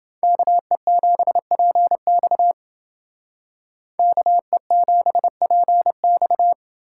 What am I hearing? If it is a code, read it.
Morse code, heard as a keyed tone, about 30 wpm, 700 Hz. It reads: KE7PX KE7PX